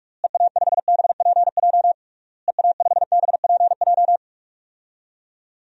Morse code, heard as a keyed tone, about 46 wpm, 700 Hz. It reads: EA5BPJ EA5BPJ
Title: DXCC-100 Call Signs - 6 Characters - Repeated 3x - 45wpm